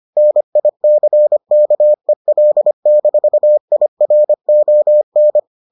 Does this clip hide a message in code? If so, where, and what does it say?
Yes: Morse code from 0.2 to 5.4 s, heard as a keyed tone, about 25 wpm, 600 Hz. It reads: NICKEL-IRON